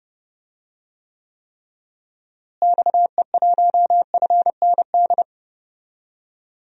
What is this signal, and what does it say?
Morse code, heard as a keyed tone, about 30 wpm, 700 Hz. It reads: XE1FND
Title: DXCC-100 Call Signs - 6 Characters - Repeated 3x - 30wpm